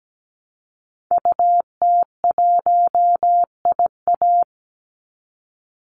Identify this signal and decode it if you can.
Morse code, heard as a keyed tone, about 17 wpm, 700 Hz. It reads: UT1IA
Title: Call Signs Repeated 3x - 17wpm